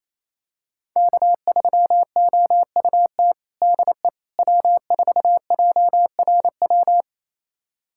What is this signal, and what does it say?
Morse code, heard as a keyed tone, about 28 wpm, 700 Hz. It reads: K3OUT DE W4JRW
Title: USA Callsign DE Callsign - 5 Characters - 28wpm